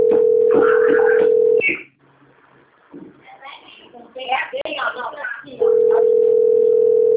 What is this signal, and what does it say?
Ringback tone (North America), 440 + 480 Hz, with speech.
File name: rg-1600-unknown- - - .46704